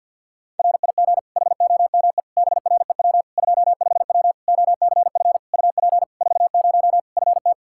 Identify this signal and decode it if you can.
Morse code, heard as a keyed tone, about 53 wpm, 700 Hz. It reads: RIG HOME BREW 25W OCF UP 40 FT
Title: Practice QSO - Exchange 2 - Info - 50wpm